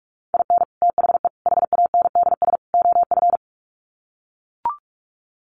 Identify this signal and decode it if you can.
Morse code, heard as a keyed tone, about 44 wpm, 700 Hz. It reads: IN THE HANDS OF
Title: Sets of 4 Words 45wpm